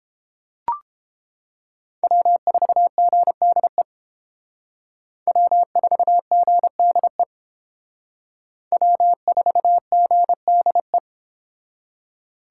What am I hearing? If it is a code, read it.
Morse code, heard as a keyed tone, about 28 wpm, 700 Hz. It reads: W4GDE W4GDE W4GDE